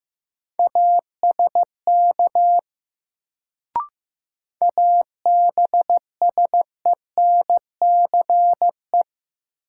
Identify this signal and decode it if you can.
Morse code, heard as a keyed tone, about 15 wpm, 700 Hz. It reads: ASK ABSENCE
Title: Sets of 4 Words - New Word Drill - 15wpm